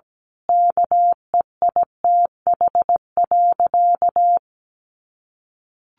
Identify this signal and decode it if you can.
Morse code, heard as a keyed tone, about 17 wpm, 700 Hz. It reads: KEITH.
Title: A Sound Of Thunder 17wpm